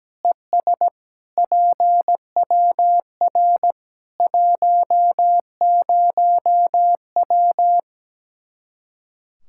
Morse code, heard as a keyed tone, about 17 wpm, 700 Hz. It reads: ES PWR 10W
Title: Practice QSO - Exchange 2 - Info - 17wpm